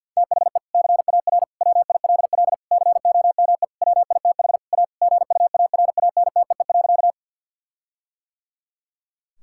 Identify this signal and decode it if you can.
Morse code, heard as a keyed tone, about 50 wpm, 700 Hz. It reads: THE CAR WILL COME WITH A GUARANTEE.
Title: Sentences from Top 1600 Words 50wpm